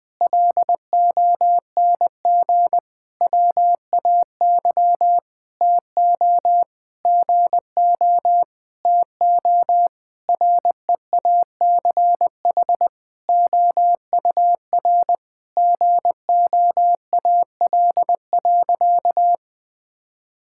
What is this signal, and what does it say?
Morse code, heard as a keyed tone, about 20 wpm, 700 Hz. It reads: LONG WAY TO GO TO REACH OUR GOAL.
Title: Sentences with 7-Word Phrases - 20wpm